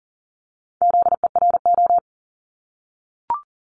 Morse code, heard as a keyed tone, about 38 wpm, 690 Hz. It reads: ZERO
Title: Multi-letter Phonemes 40wpm